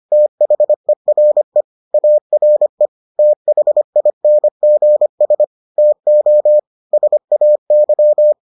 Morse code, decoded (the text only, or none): THERE ARE THINGS TO SAY